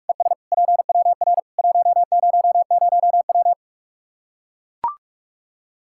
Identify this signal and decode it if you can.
Morse code, heard as a keyed tone, about 44 wpm, 700 Hz. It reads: ES PWR 100W
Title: K1USN SST QSO Practice 45wpm